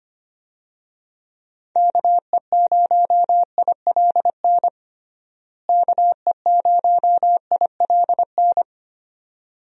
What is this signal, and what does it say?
Morse code, heard as a keyed tone, about 25 wpm, 700 Hz. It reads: KE0ILN KE0ILN
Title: USA Call Signs - 6 Characters - Repeated 2x - 25wpm